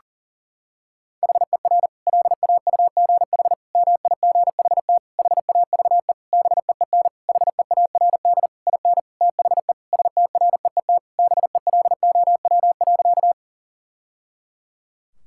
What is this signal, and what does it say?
Morse code, heard as a keyed tone, about 40 wpm, 700 Hz. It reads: HER LAUGH MIGHT HAVE BEEN HEARD IN THE STREET BELOW.
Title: Sentences from Top 1000 Words - Review Part 5 - 40wpm